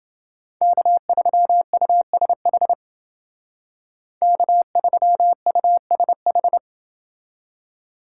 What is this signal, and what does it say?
Morse code, heard as a keyed tone, about 28 wpm, 700 Hz. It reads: K3USH K3USH